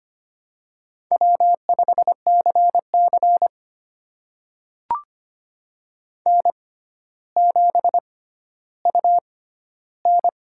Morse code, heard as a keyed tone, about 25 wpm, spaced as for 13 wpm, 700 Hz. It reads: W5CC N7UN
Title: USA Call Signs - 4 Characters - 6x Character Spacing - 25wpm